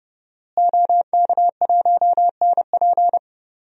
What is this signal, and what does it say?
Morse code, heard as a keyed tone, about 30 wpm, 700 Hz. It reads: OK1NP
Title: DXCC-100 Call Signs - 5 Characters - Repeated 3x - 30wpm